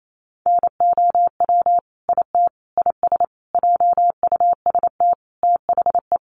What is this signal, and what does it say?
Morse code, heard as a keyed tone, about 28 wpm, 700 Hz. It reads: NOW IT IS JUST THE
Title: Sentences from Top 1300 Words 28wpm